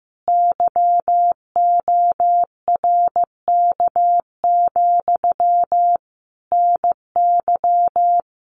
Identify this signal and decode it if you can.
Morse code, heard as a keyed tone, about 15 wpm, 700 Hz. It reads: YORK, NY
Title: Common QSO Phrases Encore 15wpm